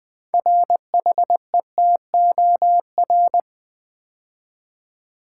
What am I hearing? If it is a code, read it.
Morse code, heard as a keyed tone, about 20 wpm, 700 Hz. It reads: RHETOR
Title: Build A Word - 3 Letters At A Time - 8-Letter Words - 20wpm